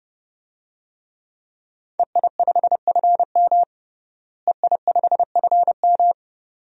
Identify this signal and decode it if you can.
Morse code, heard as a keyed tone, about 32 wpm, 700 Hz. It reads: EI5FM EI5FM
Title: DXCC-100 Call Signs - 5 Characters - Repeated 2x - 30wpm